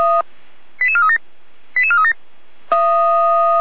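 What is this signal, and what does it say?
Five-tone selective call (ZVEI), heard as quick runs of five tones, tones of about 70 ms. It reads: 88437 88437
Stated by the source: Five-tone ZVEI selective call received by SDR, from audio.f85360.feuerwehr-orlamuende.org (Feuerwehr Orlamünde fire brigade).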